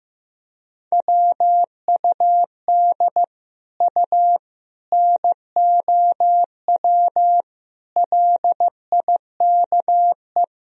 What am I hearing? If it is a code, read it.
Morse code, heard as a keyed tone, about 15 wpm, 700 Hz. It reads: WUD U NOW LIKE
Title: Sentences from Top 200 Words and Abbreviations 15wpm